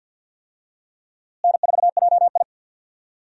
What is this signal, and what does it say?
Morse code, heard as a keyed tone, about 51 wpm, 700 Hz. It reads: N4JI
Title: USA Call Signs - 4 Characters - Repeated 2x - 50wpm